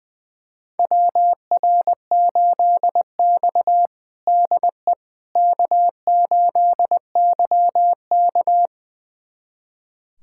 Morse code, heard as a keyed tone, about 20 wpm, 700 Hz. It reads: WR8X DE K8YK
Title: USA Callsign DE Callsign - 4 Characters - 20wpm